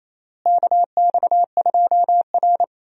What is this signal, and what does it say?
Morse code, heard as a keyed tone, about 28 wpm, 700 Hz. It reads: KX2R
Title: USA Call Signs - 4 Characters - Repeated 3x - 28wpm